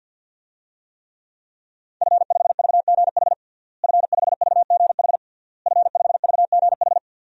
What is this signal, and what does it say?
Morse code, heard as a keyed tone, about 52 wpm, 700 Hz. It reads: F5VGH F5VGH F5VGH
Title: DXCC-100 Call Signs - 5 Characters - Repeated 3x - 50wpm